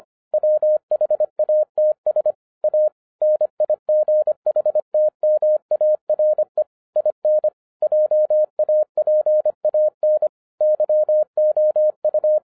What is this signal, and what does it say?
Morse code, heard as a keyed tone, about 25 wpm, 610 Hz. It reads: WHATS A NIGHTMARE IN JAPAN YOU